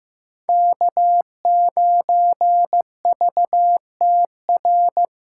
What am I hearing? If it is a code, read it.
Morse code, heard as a keyed tone, about 15 wpm, 700 Hz. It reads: K9VTR
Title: SOTA QSO Practice 15wpm